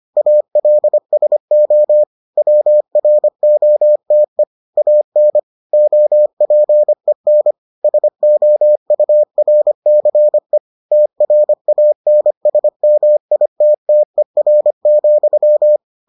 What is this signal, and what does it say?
Morse code, heard as a keyed tone, about 25 wpm, 600 Hz. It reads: ALSO WROTE AN OPEN SOURCE TRANSMITTER,